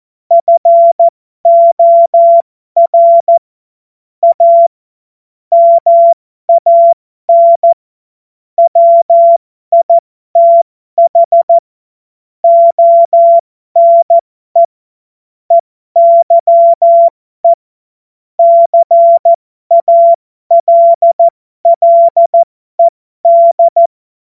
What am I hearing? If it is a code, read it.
Morse code, heard as a keyed tone, about 14 wpm, spaced as for 12 wpm, 680 Hz. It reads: FOR A MAN WITH ONE EYE CALLED
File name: Morsekurs / Witze